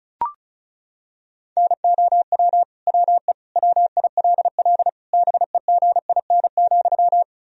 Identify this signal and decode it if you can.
Morse code, heard as a keyed tone, about 34 wpm, 700 Hz. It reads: NOW WE WILL BEGIN,
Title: Sentences from Top 1000 Words - Review Part 1 - 35wpm